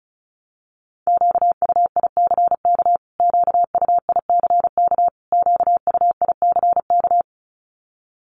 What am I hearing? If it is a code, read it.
Morse code, heard as a keyed tone, about 34 wpm, 700 Hz. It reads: QUICK QUICK QUICK
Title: Top 500 Words Repeated 35wpm